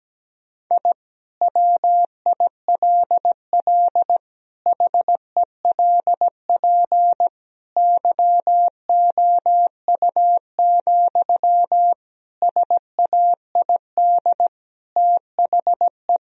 Morse code, heard as a keyed tone, about 17 wpm, 700 Hz. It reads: I WILL HELP YOU, SAID THE
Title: Sentences from Top 1000 Words - Review Part 3 - 17wpm